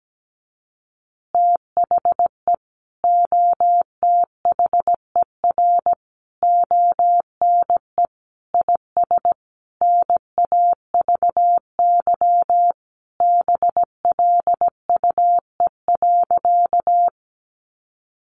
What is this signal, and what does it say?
Morse code, heard as a keyed tone, about 17 wpm, 700 Hz. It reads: THE OTHER ONE IS NAVY BLUE.
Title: Additional Easy Phrases 17wpm